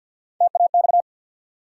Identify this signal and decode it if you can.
Morse code, heard as a keyed tone, about 48 wpm, 700 Hz. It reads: TAX
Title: Three Letter Words - 50wpm